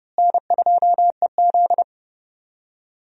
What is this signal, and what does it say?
Morse code, heard as a keyed tone, about 32 wpm, 700 Hz. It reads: N2EZ
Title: USA Call Signs - 4 Characters - Repeated 3x - 30wpm